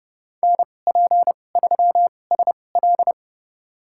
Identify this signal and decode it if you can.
Morse code, heard as a keyed tone, about 29 wpm, spaced as for 21 wpm, 700 Hz. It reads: NP3SL
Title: USA Call Signs - 5 Characters - 2x Character Spacing - 30wpm